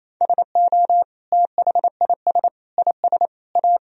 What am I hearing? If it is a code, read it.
Morse code, heard as a keyed tone, about 28 wpm, 700 Hz. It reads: SO THIS IS A
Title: Sentences from Top 1000 Words - Review Part 1 - 28wpm